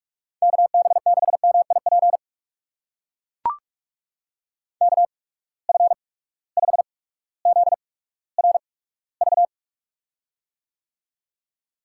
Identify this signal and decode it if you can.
Morse code, heard as a keyed tone, about 44 wpm, spaced as for 19 wpm, 700 Hz. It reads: KB6MIP KF5ZRV
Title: USA Call Signs - 8x Character Spacing - 45wpm